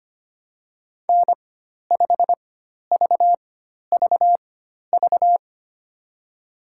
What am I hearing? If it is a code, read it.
Morse code, heard as a keyed tone, about 25 wpm, spaced as for 12 wpm, 700 Hz. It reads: N5VVV
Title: USA Call Signs - 5 Characters - 4x Character Spacing - 25wpm